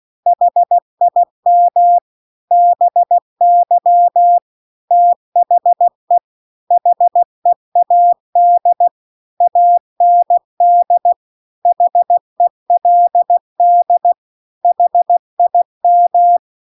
Morse code, heard as a keyed tone, about 16 wpm, 700 Hz. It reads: HIM BY THE HEAD AND HELD HIM